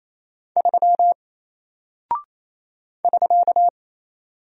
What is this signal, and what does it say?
Morse code, heard as a keyed tone, about 27 wpm, 700 Hz. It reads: 3 <SK>
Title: Single Character (Letter:Number:Punctuation:Prosign) 28wpm